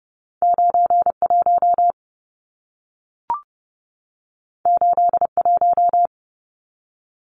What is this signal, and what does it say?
Morse code, heard as a keyed tone, about 30 wpm, 700 Hz. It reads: 91 81
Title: Numbers - 2-Digit 30wpm